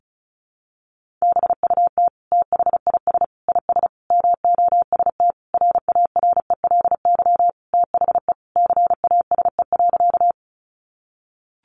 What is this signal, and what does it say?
Morse code, heard as a keyed tone, about 34 wpm, 700 Hz. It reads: BUT THIS IS MOST RARELY THE CASE.
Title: Sentences from Top 1400 Words 35wpm